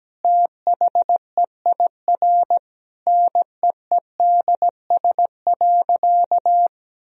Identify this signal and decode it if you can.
Morse code, heard as a keyed tone, about 17 wpm, 700 Hz. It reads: THEIR NEEDS.
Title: Sentences from Top 1000 Words - Review Part 3 - 17wpm